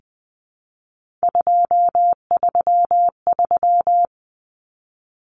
Morse code, heard as a keyed tone, about 20 wpm, 700 Hz. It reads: 233
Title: Numbers - 3-Digit 20wpm